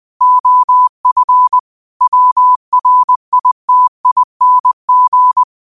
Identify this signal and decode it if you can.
Morse code, heard as a keyed tone, about 20 wpm, 1000 Hz. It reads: OF WRITING